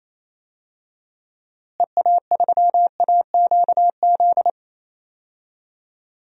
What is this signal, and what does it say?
Morse code, heard as a keyed tone, about 28 wpm, 700 Hz. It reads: EA3AQZ
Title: DXCC-100 Call Signs - 6 Characters - Repeated 2x - 28wpm